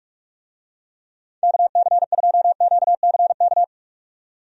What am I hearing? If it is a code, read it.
Morse code, heard as a keyed tone, about 46 wpm, 700 Hz. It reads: KC2QCK
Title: USA Call Signs - 6 Characters - 45wpm